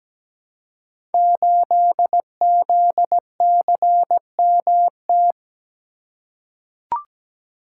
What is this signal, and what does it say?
Morse code, heard as a keyed tone, about 17 wpm, 700 Hz. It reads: 8ZCMT
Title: Code Groups - 5 Letter:Numbers - 17wpm